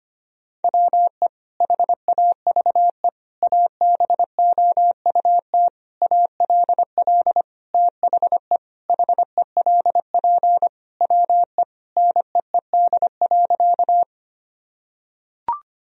Morse code, heard as a keyed tone, about 25 wpm, 700 Hz. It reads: WE HAVE ABOUT ALL THE HELP WE NEED.
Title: Sentences from Top 1000 Words - Review Part 1 - 25wpm